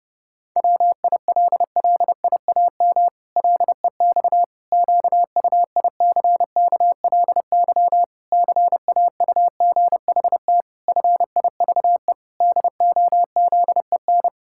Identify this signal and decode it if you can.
Morse code, heard as a keyed tone, about 30 wpm, 700 Hz. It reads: WILLIAM LEX QUICKLY CAUGHT FIVE DOZEN
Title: Pangram Challenge v2 - 30wpm